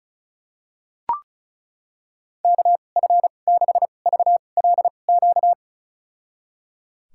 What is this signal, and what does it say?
Morse code, heard as a keyed tone, about 36 wpm, spaced as for 25 wpm, 700 Hz. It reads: KF6VLQ
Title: USA Call Signs - 6 Characters - 2x Character Spacing - 35wpm